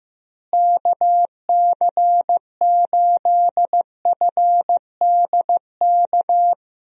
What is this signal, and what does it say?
Morse code, heard as a keyed tone, about 16 wpm, 700 Hz. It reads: KC8FDK
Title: USA Call Signs - 6 Characters - Repeated 3x - 15wpm